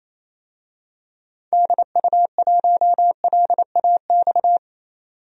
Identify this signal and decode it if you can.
Morse code, heard as a keyed tone, about 27 wpm, 700 Hz. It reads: DU1LAX
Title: DXCC-100 Call Signs - 6 Characters - 28wpm